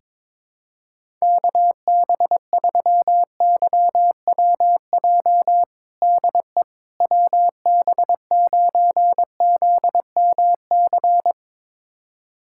Morse code, heard as a keyed tone, about 22 wpm, 700 Hz. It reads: KB3YWJ DE WB9ZMC
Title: USA Callsign DE Callsign 22wpm